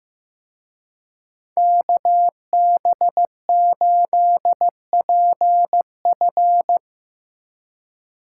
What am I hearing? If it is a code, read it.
Morse code, heard as a keyed tone, about 16 wpm, 700 Hz. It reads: KB8PF
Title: Speed Racing - USA 2x2 Call Signs 15wpm